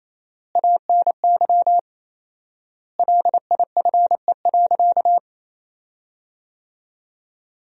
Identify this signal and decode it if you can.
Morse code, heard as a keyed tone, about 28 wpm, 700 Hz. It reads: ANY LIFE.
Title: 4x Word-Spacing - Sentences from Top 300 Words - 28wpm